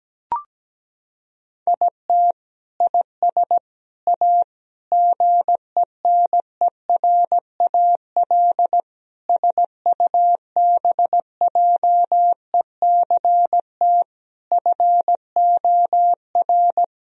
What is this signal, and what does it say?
Morse code, heard as a keyed tone, about 17 wpm, 700 Hz. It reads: IT IS A GENERAL SUBJECT FOR
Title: Sentences from Top 1000 Words - Review Part 3 - 17wpm